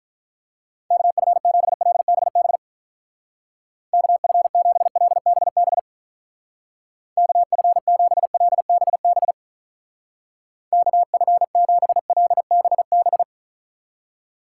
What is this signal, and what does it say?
Morse code, heard as a keyed tone, about 43 wpm, 700 Hz. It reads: KF7LBB KF7LBB KF7LBB KF7LBB
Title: Speed Racing - USA 2x3 Call Signs 35wpm